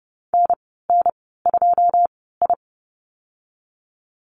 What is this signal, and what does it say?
Morse code, heard as a keyed tone, about 30 wpm, spaced as for 17 wpm, 700 Hz. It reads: NN2I
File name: Lesson 15g - Number 5 - Call Signs 30wpm